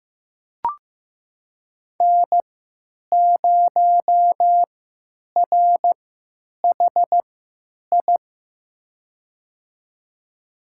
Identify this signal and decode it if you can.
Morse code, heard as a keyed tone, about 16 wpm, spaced as for 9 wpm, 700 Hz. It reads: N0RHI
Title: USA Call Signs - 3x Character Spacing - 15wpm